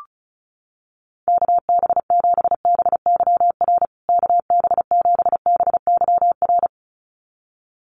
Morse code, heard as a keyed tone, about 34 wpm, 700 Hz. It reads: KB7BYR KB7BYR
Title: USA Call Signs Repeated 2x 35wpm